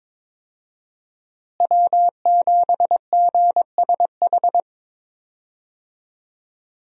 Morse code, heard as a keyed tone, about 22 wpm, 700 Hz. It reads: W7GSH
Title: USA Call Signs - 5 Characters - Repeated 2x - 22wpm